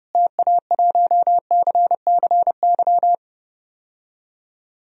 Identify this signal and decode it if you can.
Morse code, heard as a keyed tone, about 30 wpm, 700 Hz. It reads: TA1CCY
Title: DXCC-100 Call Signs - 6 Characters - Repeated 2x - 30wpm